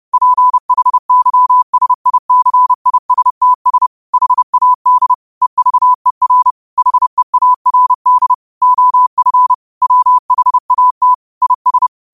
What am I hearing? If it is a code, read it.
Morse code, heard as a keyed tone, about 30 wpm, 1000 Hz. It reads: PHYSICISTS HAD EVER HEARD OF WHAT IS
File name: qotdfiles / 30wpm